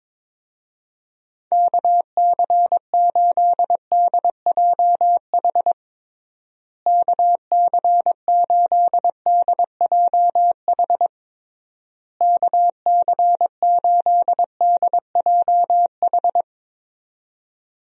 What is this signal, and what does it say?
Morse code, heard as a keyed tone, about 21 wpm, 700 Hz. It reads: KC8DJH KC8DJH KC8DJH
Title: USA Call Signs - 6 Characters - Repeated 3x - 22wpm